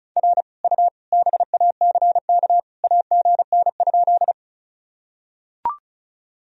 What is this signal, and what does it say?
Morse code, heard as a keyed tone, about 36 wpm, 700 Hz. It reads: R U BACK AGN?
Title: Sentences from Top 200 Words and Abbreviations 35wpm